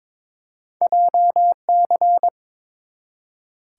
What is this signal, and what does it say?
Morse code, heard as a keyed tone, about 22 wpm, 700 Hz. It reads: JC
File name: practice / 22wpm / Two Letter-Number Combinations - 22wpm